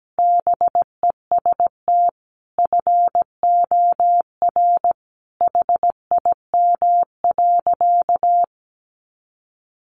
Morse code, heard as a keyed tone, about 17 wpm, 700 Hz. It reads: BEST FOR HIM.
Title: Sentences from Top 500 Words 17wpm